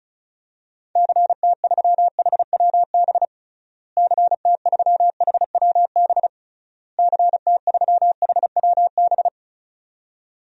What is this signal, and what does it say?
Morse code, heard as a keyed tone, about 34 wpm, 700 Hz. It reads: CT3HWB CT3HWB CT3HWB
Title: DXCC-100 Call Signs - 6 Characters - Repeated 3x - 35wpm